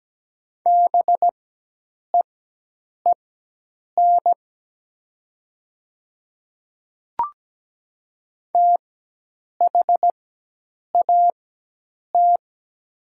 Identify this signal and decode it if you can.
Morse code, heard as a keyed tone, about 17 wpm, spaced as for 8 wpm, 700 Hz. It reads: BEEN THAT
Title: 4x Character Spacing - Top 100 Words - 17wpm